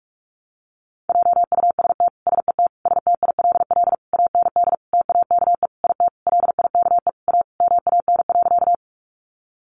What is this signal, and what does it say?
Morse code, heard as a keyed tone, about 44 wpm, 700 Hz. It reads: JUST SET STILL AND TAKE IT LIKE A MAN.
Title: Sentences from Top 300 Words 45wpm